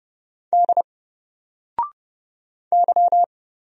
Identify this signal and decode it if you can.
Morse code, heard as a keyed tone, about 30 wpm, 700 Hz. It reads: D Y
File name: Single-Letter-Number-30wpm